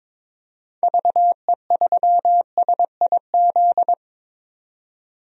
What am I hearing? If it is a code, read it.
Morse code, heard as a keyed tone, about 22 wpm, 700 Hz. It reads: VE3SIZ
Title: DXCC-100 Call Signs - 6 Characters - 22wpm